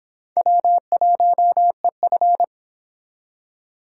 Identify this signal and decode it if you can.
Morse code, heard as a keyed tone, about 26 wpm, 700 Hz. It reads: W1EF